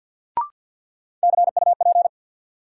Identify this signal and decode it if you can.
Morse code, heard as a keyed tone, about 52 wpm, 700 Hz. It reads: CUP
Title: Sentences from Top 1600 Words - New Word Drill - 50wpm